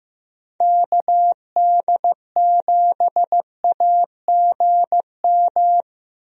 Morse code, heard as a keyed tone, about 15 wpm, 700 Hz. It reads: KD7AGM
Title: USA Call Signs - 6 Characters - Repeated 2x - 15wpm